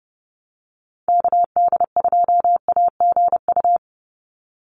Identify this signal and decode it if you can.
Morse code, heard as a keyed tone, about 30 wpm, 700 Hz. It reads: KD2AGU
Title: USA Call Signs 30wpm